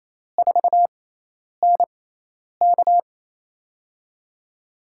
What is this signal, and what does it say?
Morse code, heard as a keyed tone, about 28 wpm, spaced as for 10 wpm, 700 Hz. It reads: 4NK
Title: USA Call Signs - 5 Characters - 6x Character Spacing - 28wpm